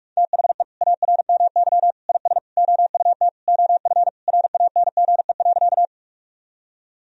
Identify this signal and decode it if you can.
Morse code, heard as a keyed tone, about 46 wpm, 700 Hz. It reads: THE ARMY IS OUT OF RANGE.
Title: Sentences from Top 1000 Words - Review Part 4 - 45wpm